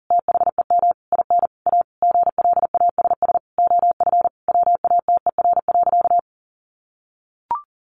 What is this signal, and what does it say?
Morse code, heard as a keyed tone, about 40 wpm, 700 Hz. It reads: THEM IN A GLASS OF WATER.
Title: Sentences from Top 1300 Words 40wpm